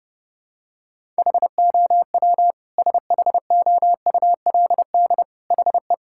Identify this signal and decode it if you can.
Morse code, heard as a keyed tone, about 30 wpm, 700 Hz. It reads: HOW SHOULD HE
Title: Sentences from Top 1000 Words - Review Part 6 - 30wpm